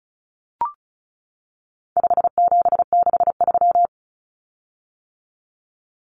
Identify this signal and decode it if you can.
Morse code, heard as a keyed tone, about 34 wpm, 700 Hz. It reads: 5763
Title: Common Cut Numbers - 4-Digit 35wpm